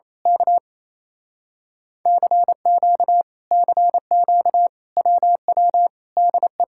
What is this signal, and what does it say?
Morse code, heard as a keyed tone, about 27 wpm, 700 Hz. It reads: K CQ CQ WW DE
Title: Speed-Racing Common QSO Phrases - 28wpm